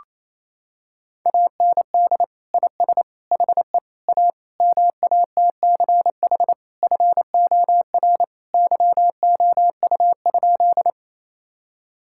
Morse code, heard as a keyed tone, about 28 wpm, 700 Hz. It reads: AND IS HE A MATCH FOR YOU?
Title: Sentences from Top 1600 Words 28wpm